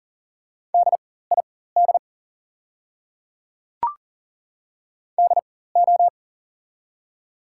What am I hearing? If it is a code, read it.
Morse code, heard as a keyed tone, about 39 wpm, spaced as for 19 wpm, 690 Hz. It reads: DID DO
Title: 4x Character Spacing - Top 100 Words - 40wpm